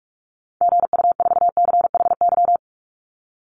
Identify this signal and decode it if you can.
Morse code, heard as a keyed tone, about 44 wpm, 700 Hz. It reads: GU4CHY
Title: Call Signs - 45wpm